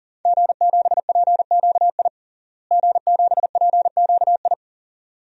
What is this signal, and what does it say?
Morse code, heard as a keyed tone, about 40 wpm, 700 Hz. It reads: G7PQI G7PQI
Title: DXCC-100 Call Signs - 5 Characters - Repeated 2x - 40wpm